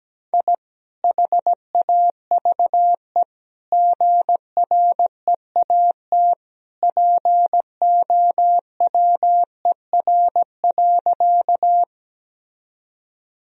Morse code, heard as a keyed tone, about 17 wpm, 700 Hz. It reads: I HAVE GREAT POWER.
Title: Sentences from Top 1000 Words - Review Part 2 - 17wpm